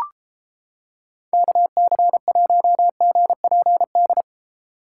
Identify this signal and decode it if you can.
Morse code, heard as a keyed tone, about 31 wpm, 700 Hz. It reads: KC1GPD